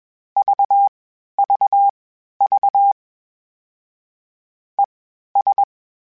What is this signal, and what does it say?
Morse code, heard as a keyed tone, about 21 wpm, spaced as for 12 wpm, 800 Hz. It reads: VVV ES
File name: General / General-mp3s / Suffixes sp @20